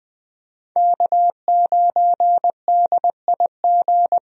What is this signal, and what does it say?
Morse code, heard as a keyed tone, about 21 wpm, 700 Hz. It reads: K9DIG
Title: USA Call Signs - 5 Characters - Repeated 3x - 20wpm